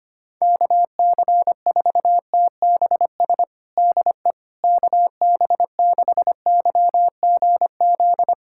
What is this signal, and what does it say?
Morse code, heard as a keyed tone, about 25 wpm, 700 Hz. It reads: KC4TBS DE KB6YGZ
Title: USA Callsign DE Callsign - 6 Characters - 25wpm